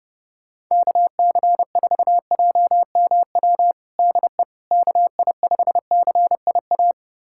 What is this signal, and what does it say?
Morse code, heard as a keyed tone, about 30 wpm, 700 Hz. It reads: KC4JMW DE KI5CIA
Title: USA Callsign DE Callsign 30wpm